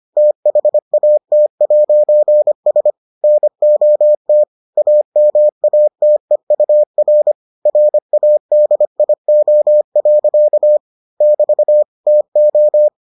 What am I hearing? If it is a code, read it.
Morse code, heard as a keyed tone, about 25 wpm, 600 Hz. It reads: THAT'S NOT AMATEUR RADIO. <BT> TO